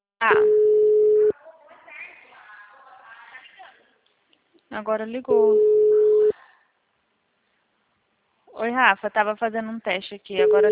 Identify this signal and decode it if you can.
Ringback tone (Europe, 425 Hz), with speech.
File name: out- -2048- - - .38917